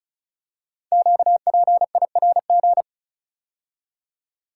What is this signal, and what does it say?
Morse code, heard as a keyed tone, about 36 wpm, 700 Hz. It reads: QPIRG
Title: Code Groups - 5 Letters - 35wpm